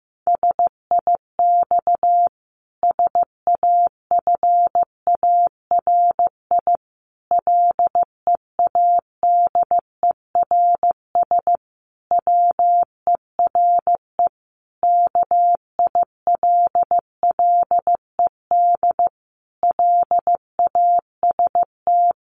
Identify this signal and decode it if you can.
Morse code, heard as a keyed tone, about 15 wpm, 700 Hz. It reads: SIX SAFARI LEADERS WERE KILLED LAST
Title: A Sound Of Thunder 15wpm